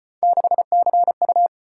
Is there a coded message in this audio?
Yes: Morse code from 0.2 to 1.5 s, heard as a keyed tone, about 34 wpm, 700 Hz. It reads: 6CU